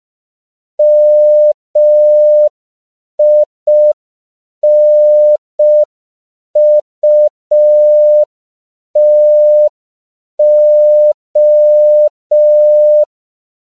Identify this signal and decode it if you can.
Morse code, heard as a keyed tone, about 5 wpm, 600 Hz. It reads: MINUTO
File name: CW / 5PPM